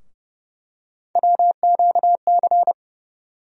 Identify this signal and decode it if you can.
Morse code, heard as a keyed tone, about 30 wpm, 700 Hz. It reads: WQC
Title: USA Call Sign Postfixes - 3 Letters 30wpm